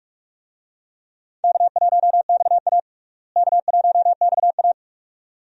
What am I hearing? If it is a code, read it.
Morse code, heard as a keyed tone, about 46 wpm, 700 Hz. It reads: K1XA K1XA
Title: USA Call Signs - 4 Characters - Repeated 2x - 45wpm